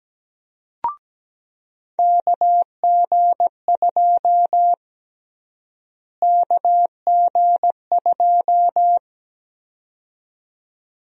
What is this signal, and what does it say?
Morse code, heard as a keyed tone, about 17 wpm, 700 Hz. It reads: KG2 KG2
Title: USA Call Sign Prefixes - Repeated 2x - 17wpm